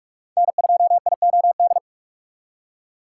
Morse code, heard as a keyed tone, about 46 wpm, 700 Hz. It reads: N2IOD
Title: USA Call Signs - 2x Character Spacing - 45wpm